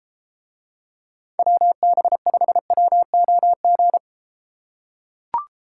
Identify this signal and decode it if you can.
Morse code, heard as a keyed tone, about 32 wpm, 700 Hz. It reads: WB5WOG